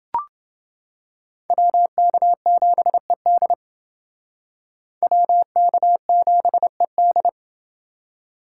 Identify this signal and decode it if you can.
Morse code, heard as a keyed tone, about 28 wpm, 700 Hz. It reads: WK7ED WK7ED